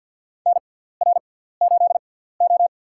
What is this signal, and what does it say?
Morse code, heard as a keyed tone, about 53 wpm, spaced as for 17 wpm, 700 Hz. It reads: NR8O
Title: USA Call Signs - 4 Characters - 6x Character Spacing - 50wpm